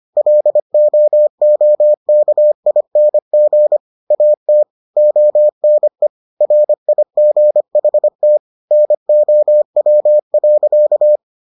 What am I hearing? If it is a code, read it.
Morse code, heard as a keyed tone, about 25 wpm, 600 Hz. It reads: LOOKING AT ONE RIGHT NOW.